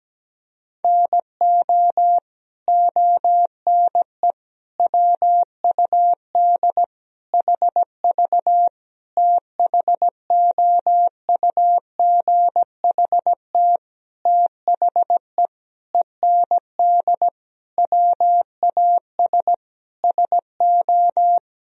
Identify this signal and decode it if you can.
Morse code, heard as a keyed tone, about 17 wpm, 700 Hz. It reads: NO ONE WUD HV THOUGHT THE END WAS SO
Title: Sentences from Top 300 Words and Abbreviations 17wpm